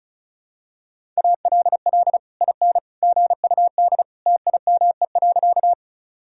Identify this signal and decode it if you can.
Morse code, heard as a keyed tone, about 34 wpm, 700 Hz. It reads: ALL IN GUD TIME.
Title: Sentences from Top 200 Words and Abbreviations 35wpm